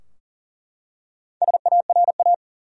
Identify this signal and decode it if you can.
Morse code, heard as a keyed tone, about 40 wpm, 700 Hz. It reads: SARA
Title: USA Common Names - Max 5 Letters - 40wpm